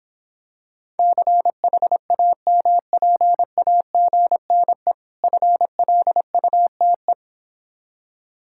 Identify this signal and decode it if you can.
Morse code, heard as a keyed tone, about 26 wpm, 700 Hz. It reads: CHAMPAGNE FLUTE